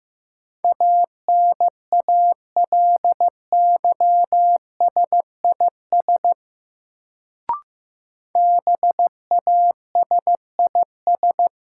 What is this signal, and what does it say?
Morse code, heard as a keyed tone, about 15 wpm, 700 Hz. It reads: ANALYSIS BASIS
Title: Sentences from Top 1000 Words - Review Part 5 - New Word Drill - 15wpm